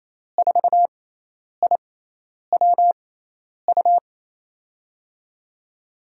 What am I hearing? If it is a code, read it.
Morse code, heard as a keyed tone, about 28 wpm, spaced as for 10 wpm, 700 Hz. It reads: 4IWU
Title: USA Call Signs - 6 Characters - 6x Character Spacing - 28wpm